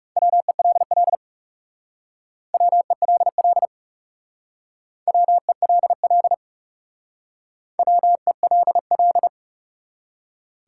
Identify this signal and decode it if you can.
Morse code, heard as a keyed tone, about 38 wpm, 700 Hz. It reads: WELL WELL WELL WELL
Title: Speed-Racing Top 180 QSO Elements 30wpm